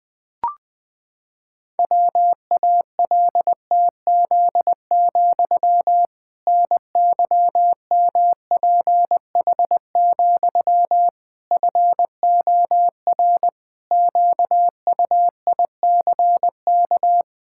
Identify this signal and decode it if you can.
Morse code, heard as a keyed tone, about 20 wpm, 700 Hz. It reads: WALTZ, NYMPH, FOR QUICK
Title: Pangram Challenge v2 - 20wpm